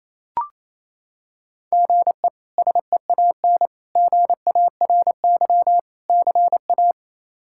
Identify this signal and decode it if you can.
Morse code, heard as a keyed tone, about 28 wpm, 700 Hz. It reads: GE SEAN GARY CA
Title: K1USN SST QSO Practice - 28wpm